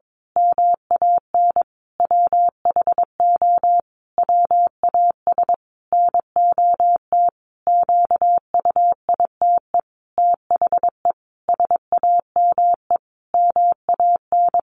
Morse code, heard as a keyed tone, about 22 wpm, 700 Hz. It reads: MAN WHO WAS NOT QUITE THE SAME MAN
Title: A Sound Of Thunder 22wpm